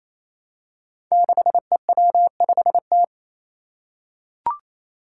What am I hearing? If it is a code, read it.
Morse code, heard as a keyed tone, about 28 wpm, 700 Hz. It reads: 6EW5T
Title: Five Challenging Letter-Number Combinations 28wpm